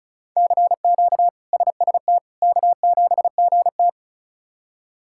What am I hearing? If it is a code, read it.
Morse code, heard as a keyed tone, about 34 wpm, 700 Hz. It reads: CQ SST K7GT
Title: K1USN SST QSO Practice - 35wpm